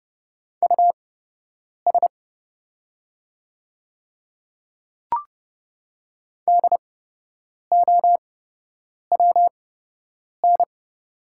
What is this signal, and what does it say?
Morse code, heard as a keyed tone, about 30 wpm, spaced as for 8 wpm, 690 Hz. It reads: US DOWN
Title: 8x Character Spacing - Top 200 Words - 30wpm